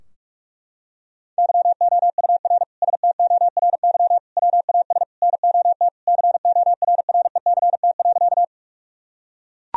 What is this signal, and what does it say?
Morse code, heard as a keyed tone, about 42 wpm, 700 Hz. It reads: YOUR STORY WAS NOT CORRECT.
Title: Sentences from Top 1000 Words - Review Part 2 - 45wpm